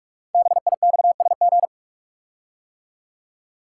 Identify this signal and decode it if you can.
Morse code, heard as a keyed tone, about 44 wpm, 700 Hz. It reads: BIXSG
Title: Code Groups - 5 Letters - 45wpm